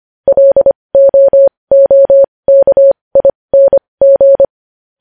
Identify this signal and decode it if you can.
Morse code, heard as a keyed tone, about 25 wpm, spaced as for 20 wpm, 550 Hz. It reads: LOOKING
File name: podcast / News headlines-1x-25x20